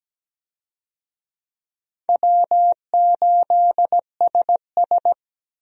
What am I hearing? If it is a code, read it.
Morse code, heard as a keyed tone, about 17 wpm, 700 Hz. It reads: W8SS
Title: USA Call Signs - 4 Characters - 17wpm